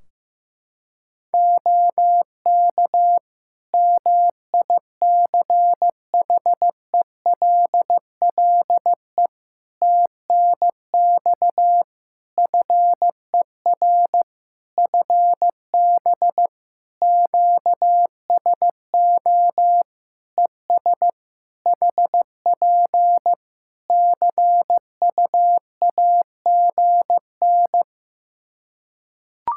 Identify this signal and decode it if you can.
Morse code, heard as a keyed tone, about 15 wpm, 700 Hz. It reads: OK MICHELLE TNX FER FB QSO ES HP CUAGN
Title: Practice QSO - Exchange 4 - Ending - 15wpm